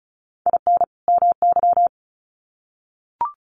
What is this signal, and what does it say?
Morse code, heard as a keyed tone, about 34 wpm, 700 Hz. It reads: IN MY
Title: Sets of 2 Words 35wpm